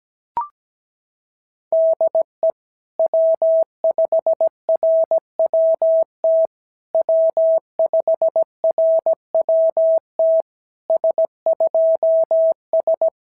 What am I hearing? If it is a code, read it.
Morse code, heard as a keyed tone, about 17 wpm, 650 Hz. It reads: DE W5RWT W5RWT S2S
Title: SOTA QSO Practice 17wpm